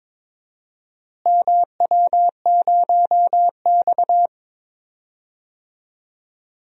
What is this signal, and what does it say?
Morse code, heard as a keyed tone, about 22 wpm, 700 Hz. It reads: MW0X
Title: DXCC-100 Call Signs - 4 Characters - Repeated 2x - 22wpm